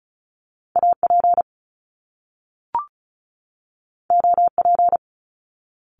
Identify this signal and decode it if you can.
Morse code, heard as a keyed tone, about 36 wpm, 700 Hz. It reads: AP OP
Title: Top Two Letter Combinations in Words 35wpm